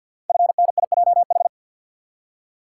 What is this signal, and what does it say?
Morse code, heard as a keyed tone, about 51 wpm, 700 Hz. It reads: FNIJH
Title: Code Groups - 5 Letters - 50wpm